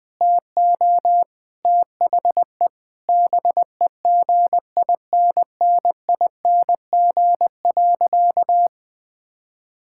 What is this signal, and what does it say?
Morse code, heard as a keyed tone, about 20 wpm, 700 Hz. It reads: TO THE BEGINNING.
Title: Sentences with 4-Word Phrases - 20wpm